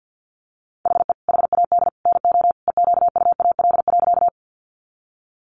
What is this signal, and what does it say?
Morse code, heard as a keyed tone, about 50 wpm, 700 Hz. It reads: HE HAD NO EQUAL.
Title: Sentences from Top 1400 Words 50wpm